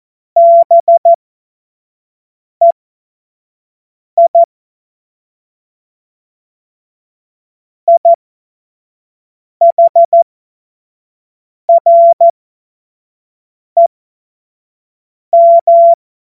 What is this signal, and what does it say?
Morse code, heard as a keyed tone, about 14 wpm, spaced as for 5 wpm, 680 Hz. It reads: BEI IHREM
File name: Morsekurs / Witze / Witz 29. November 2022 25 F70